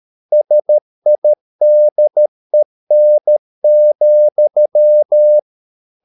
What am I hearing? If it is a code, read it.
Morse code, heard as a keyed tone, about 13 wpm, 600 Hz. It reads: SIDEN,
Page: Ferie 13 wpm